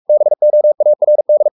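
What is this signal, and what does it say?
Morse code, heard as a keyed tone, about 44 wpm, 600 Hz. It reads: BOARD
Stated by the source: Morse code at 43 wpm sending Ken Follett's novel Never in English, from CW med Gnister.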